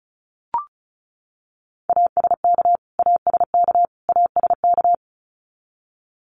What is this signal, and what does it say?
Morse code, heard as a keyed tone, about 34 wpm, 700 Hz. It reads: ASK ASK ASK
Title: Top 500 Words Repeated 35wpm